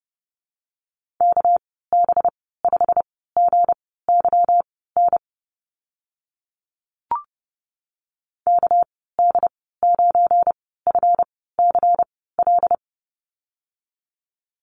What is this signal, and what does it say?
Morse code, heard as a keyed tone, about 31 wpm, spaced as for 17 wpm, 700 Hz. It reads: KB5GYN KD9FCL
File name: Lesson 37g - Letter x - Call Signs 30wpm